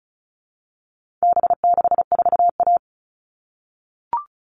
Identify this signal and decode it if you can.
Morse code, heard as a keyed tone, about 34 wpm, 700 Hz. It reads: B64A
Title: All Cut Numbers - 4-Digit 35wpm